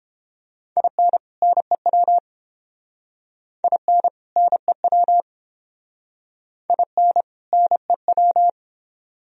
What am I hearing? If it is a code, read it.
Morse code, heard as a keyed tone, about 29 wpm, 700 Hz. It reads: IN NEW IN NEW IN NEW